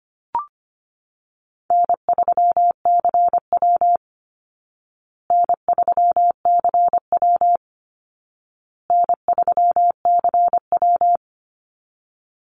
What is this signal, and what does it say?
Morse code, heard as a keyed tone, about 25 wpm, 700 Hz. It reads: N3CW N3CW N3CW
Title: Easy Call Signs Repeated 3x - 25wpm